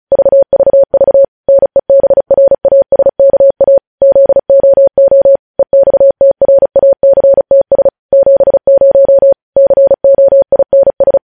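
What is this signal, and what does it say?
Morse code, heard as a keyed tone, about 36 wpm, 550 Hz. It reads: VVV NEBRASKA ZOO EXTRACTS 70 COINS
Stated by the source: News headlines sent in Morse at 35 wpm.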